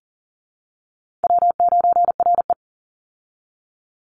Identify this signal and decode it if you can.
Morse code, heard as a keyed tone, about 40 wpm, 700 Hz. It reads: W9RE
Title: Call Signs Repeated 2x - 40wpm